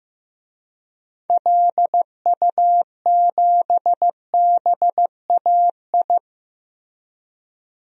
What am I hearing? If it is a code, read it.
Morse code, heard as a keyed tone, about 15 wpm, 700 Hz. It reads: LU7BAI
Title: DXCC-100 Call Signs - 6 Characters - Repeated 2x - 15wpm